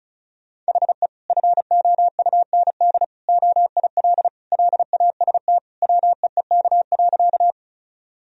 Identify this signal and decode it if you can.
Morse code, heard as a keyed tone, about 34 wpm, 700 Hz. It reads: HE FOUND OIL LAST WEEK.
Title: Sentences from Top 1000 Words - Review Part 4 - 35wpm